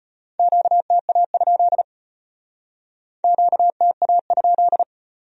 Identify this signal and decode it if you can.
Morse code, heard as a keyed tone, about 34 wpm, 700 Hz. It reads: QTA? QTA?